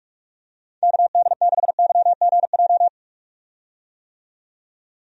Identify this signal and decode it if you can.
Morse code, heard as a keyed tone, about 42 wpm, 700 Hz. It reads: KD6YGJ
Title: USA Call Signs - 6 Characters - 45wpm